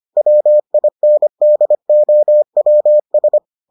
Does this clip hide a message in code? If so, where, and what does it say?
Yes: Morse code from 0.2 to 3.4 s, heard as a keyed tone, about 25 wpm, 600 Hz. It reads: WINDOWS